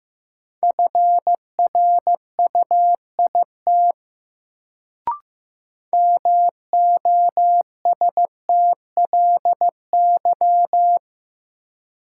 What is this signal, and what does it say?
Morse code, heard as a keyed tone, about 15 wpm, 700 Hz. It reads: FRUIT MOSTLY
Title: Sentences from Top 1300 Words - New Word Drill - 15wpm